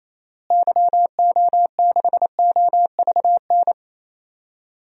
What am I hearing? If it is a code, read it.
Morse code, heard as a keyed tone, about 28 wpm, 700 Hz. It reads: YO6OVN
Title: DXCC-100 Call Signs - 6 Characters - Repeated 2x - 28wpm